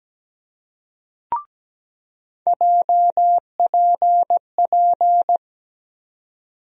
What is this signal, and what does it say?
Morse code, heard as a keyed tone, about 17 wpm, 700 Hz. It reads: JPP
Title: USA Call Sign Postfixes - 3 Letters 17wpm